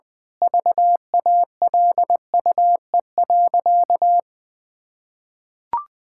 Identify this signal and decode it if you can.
Morse code, heard as a keyed tone, about 20 wpm, 700 Hz. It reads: VALUE.
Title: Sentences from Top 1000 Words - Review Part 6 - 20wpm